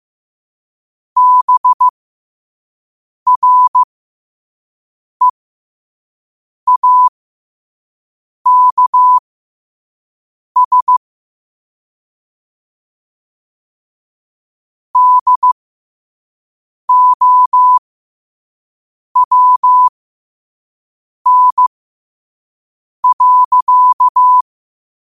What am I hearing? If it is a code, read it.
Morse code, heard as a keyed tone, about 15 wpm, spaced as for 5 wpm, 1000 Hz. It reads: BREAKS DOWN.
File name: qotdfiles / 5wpm